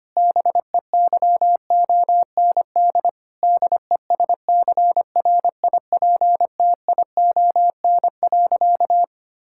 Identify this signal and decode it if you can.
Morse code, heard as a keyed tone, about 25 wpm, 700 Hz. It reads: BEYOND DESCRIPTION.
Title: Sentences from Top 1600 Words 25wpm